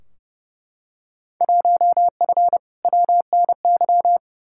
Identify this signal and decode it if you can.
Morse code, heard as a keyed tone, about 31 wpm, 700 Hz. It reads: 1F WNY
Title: Field Day Exchanges - 30wpm